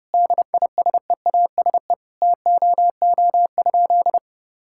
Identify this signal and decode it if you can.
Morse code, heard as a keyed tone, about 30 wpm, 700 Hz. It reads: DISEASE TOO?
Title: Sentences from Top 1000 Words - Review Part 6 - 30wpm